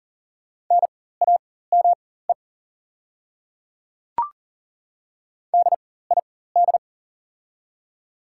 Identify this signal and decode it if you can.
Morse code, heard as a keyed tone, about 40 wpm, spaced as for 19 wpm, 690 Hz. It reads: NAME DID
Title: 4x Character Spacing - Top 200 Words - 40wpm